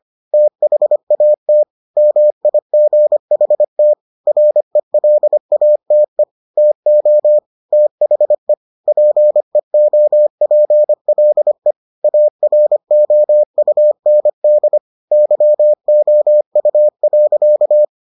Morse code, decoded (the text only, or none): THAT MIGHT RELATE TO THE PEOPLE AROUND YOU.